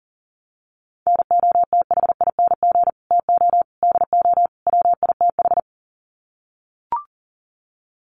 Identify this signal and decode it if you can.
Morse code, heard as a keyed tone, about 37 wpm, 700 Hz. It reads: NOTHING TO DO WITH
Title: Sets of 4 Words 40wpm